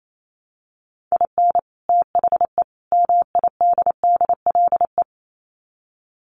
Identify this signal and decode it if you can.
Morse code, heard as a keyed tone, about 28 wpm, 700 Hz. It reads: IN THE MIDDLE
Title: Sets of 3 Words 28wpm